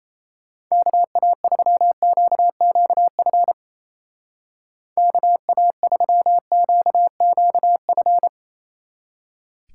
Morse code, heard as a keyed tone, about 36 wpm, 700 Hz. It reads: KA3QQF KA3QQF